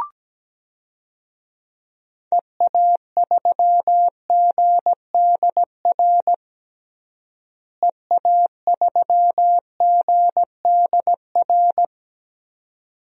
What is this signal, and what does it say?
Morse code, heard as a keyed tone, about 17 wpm, 700 Hz. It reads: EA3GDR EA3GDR
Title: DXCC-100 Call Signs - 6 Characters - Repeated 2x - 17wpm